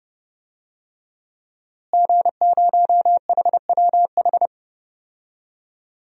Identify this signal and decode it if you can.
Morse code, heard as a keyed tone, about 32 wpm, 700 Hz. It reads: G0HWH
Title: DXCC-100 Call Signs - 5 Characters - Repeated 3x - 30wpm